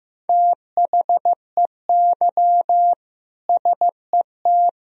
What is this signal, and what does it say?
Morse code, heard as a keyed tone, about 15 wpm, 700 Hz. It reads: THEY SET
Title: Sentences from Top 1000 Words - Review Part 3 - 15wpm